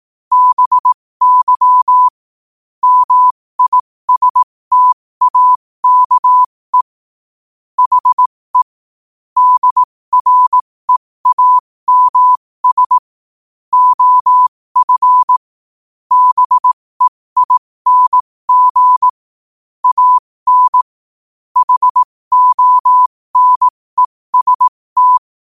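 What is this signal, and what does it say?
Morse code, heard as a keyed tone, about 18 wpm, spaced as for 15 wpm, 1000 Hz. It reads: BY MISTAKE HE DREAMS OF BEING AN HONEST
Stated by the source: Morse quote-of-the-day practice at 15 WPM from Morse Resource.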